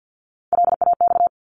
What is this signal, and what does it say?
Morse code, heard as a keyed tone, about 50 wpm, 700 Hz. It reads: LAX
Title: ARRL Sections 50wpm